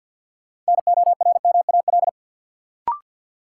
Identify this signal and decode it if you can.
Morse code, heard as a keyed tone, about 53 wpm, 700 Hz. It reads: NORMAL
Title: Sentences from Top 1200 Words - New Word Drill - 50wpm